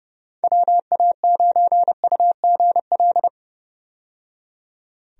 Morse code, heard as a keyed tone, about 29 wpm, 700 Hz. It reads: WA9UGL
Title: USA Call Signs - 6 Characters - 30wpm